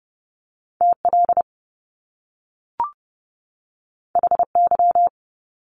Morse code, heard as a keyed tone, about 30 wpm, 700 Hz. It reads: TL HY
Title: Top Two Letter Combinations in Words 30wpm